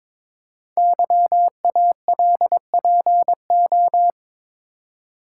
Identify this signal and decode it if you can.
Morse code, heard as a keyed tone, about 22 wpm, 700 Hz. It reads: YALPO
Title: Code Groups - 5 Letters - 22wpm